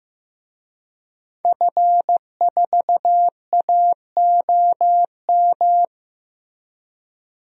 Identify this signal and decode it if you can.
Morse code, heard as a keyed tone, about 15 wpm, 700 Hz. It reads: F4AOM
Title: DXCC-100 Call Signs - 5 Characters - Repeated 2x - 15wpm